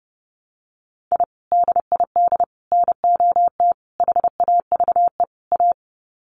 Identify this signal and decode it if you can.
Morse code, heard as a keyed tone, about 30 wpm, 700 Hz. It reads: I DID NOT HAVE A
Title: Sets of 5 Words 30wpm